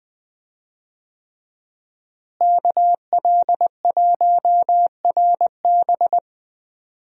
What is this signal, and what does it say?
Morse code, heard as a keyed tone, about 21 wpm, 700 Hz. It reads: KL1RB
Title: USA Call Signs - 5 Characters - Repeated 3x - 20wpm